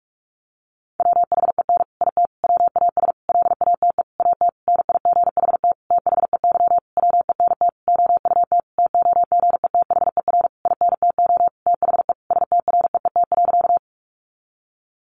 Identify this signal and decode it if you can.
Morse code, heard as a keyed tone, about 44 wpm, 700 Hz. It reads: WHEN IT WAS LATE AT NIGHT THEY WENT OUT TOGETHER INTO THE STREET.
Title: Sentences from Top 600 Words 45wpm